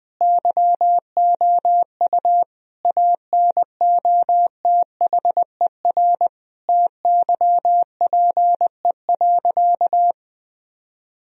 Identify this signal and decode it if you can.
Morse code, heard as a keyed tone, about 20 wpm, 700 Hz. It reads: YOU ANOTHER TYPE.
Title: Sentences from Top 1000 Words - Review Part 3 - 20wpm